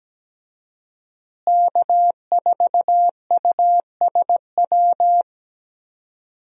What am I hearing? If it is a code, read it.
Morse code, heard as a keyed tone, about 17 wpm, 700 Hz. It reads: K4USW
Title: USA Call Signs - 5 Characters - Repeated 3x - 17wpm